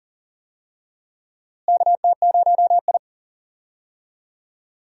Morse code, heard as a keyed tone, about 39 wpm, 700 Hz. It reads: KT0I
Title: USA Call Signs - 4 Characters - Repeated 2x - 40wpm